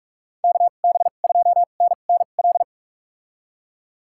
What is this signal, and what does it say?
Morse code, heard as a keyed tone, about 46 wpm, spaced as for 33 wpm, 700 Hz. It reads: KB2NNL
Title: USA Call Signs - 6 Characters - 2x Character Spacing - 45wpm